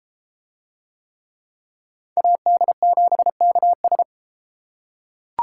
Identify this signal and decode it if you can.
Morse code, heard as a keyed tone, about 30 wpm, 700 Hz. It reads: AD7KS